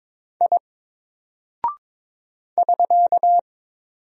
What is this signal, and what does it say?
Morse code, heard as a keyed tone, about 22 wpm, 700 Hz. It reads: I <SK>
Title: Single Character (Letter:Number:Punctuation:Prosign) 22wpm